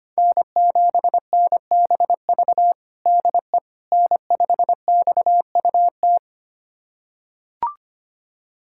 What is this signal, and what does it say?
Morse code, heard as a keyed tone, about 25 wpm, 700 Hz. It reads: N7NBV DE N5XUT
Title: USA Callsign DE Callsign - 5 Characters - 25wpm